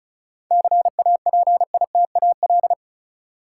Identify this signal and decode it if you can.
Morse code, heard as a keyed tone, about 36 wpm, 700 Hz. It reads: CAPITAL
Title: Sentences from Top 1000 Words - New Word Drill - 35wpm